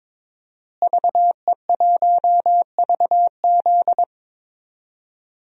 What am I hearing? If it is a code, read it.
Morse code, heard as a keyed tone, about 22 wpm, 700 Hz. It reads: VE1VZ
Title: DXCC-100 Call Signs - 5 Characters - Repeated 3x - 22wpm